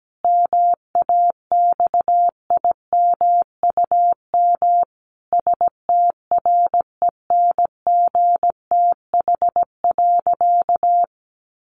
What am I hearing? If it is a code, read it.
Morse code, heard as a keyed tone, about 17 wpm, 700 Hz. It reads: MAXIMUM STRENGTH.
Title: Sentences from Top 1300 Words 17wpm